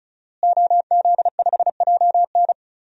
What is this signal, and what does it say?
Morse code, heard as a keyed tone, about 35 wpm, 700 Hz. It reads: OZ5JN
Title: DXCC-100 Call Signs - 5 Characters - Repeated 3x - 35wpm